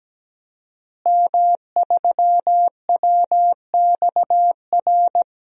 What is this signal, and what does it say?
Morse code, heard as a keyed tone, about 17 wpm, 700 Hz. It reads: M3WXR
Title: DXCC-100 Call Signs - 5 Characters - Repeated 3x - 17wpm